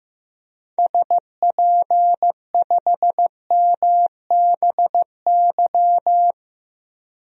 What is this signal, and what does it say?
Morse code, heard as a keyed tone, about 15 wpm, 700 Hz. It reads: SP5MBY
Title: DXCC-100 Call Signs - 6 Characters - 15wpm